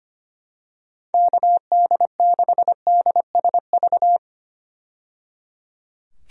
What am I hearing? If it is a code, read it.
Morse code, heard as a keyed tone, about 25 wpm, 700 Hz. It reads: KD6DSV
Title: USA Call Signs - 6 Characters - 25wpm